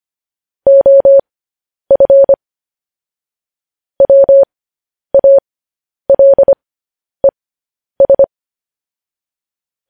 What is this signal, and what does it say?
Morse code, heard as a keyed tone, about 25 wpm, spaced as for 10 wpm, 550 Hz. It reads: OF WALES
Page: news-headlines-in-morse-code-at-10-wpm